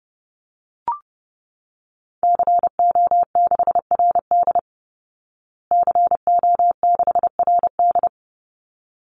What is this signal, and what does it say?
Morse code, heard as a keyed tone, about 30 wpm, 700 Hz. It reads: CO6RD CO6RD
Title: Call Signs Repeated 3x - 30wpm